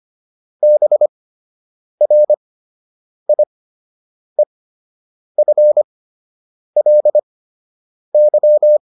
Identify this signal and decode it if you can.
Morse code, heard as a keyed tone, about 25 wpm, spaced as for 8 wpm, 600 Hz. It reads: BRIEFLY